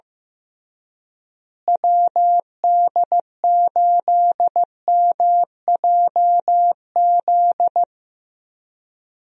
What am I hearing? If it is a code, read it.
Morse code, heard as a keyed tone, about 15 wpm, 700 Hz. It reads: WD8MJZ
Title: USA Call Signs - 6 Characters - Repeated 3x - 15wpm